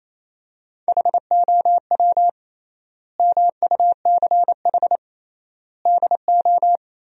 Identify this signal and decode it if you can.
Morse code, heard as a keyed tone, about 28 wpm, 700 Hz. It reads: HOW MUCH DO
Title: 3x Word-Spacing - Sentences from Top 200 Words - 28wpm